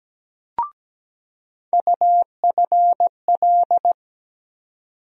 Morse code, heard as a keyed tone, about 17 wpm, 700 Hz. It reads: UFL
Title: USA Call Sign Postfixes - 3 Letters 17wpm